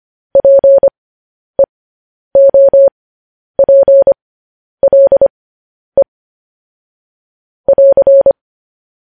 Morse code, heard as a keyed tone, about 25 wpm, spaced as for 10 wpm, 550 Hz. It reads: PEOPLE <AR>
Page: news-headlines-in-morse-code-at-10-wpm